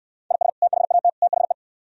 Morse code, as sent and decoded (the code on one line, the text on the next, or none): .. ... / - .... .- - / - .... .
IS THAT THE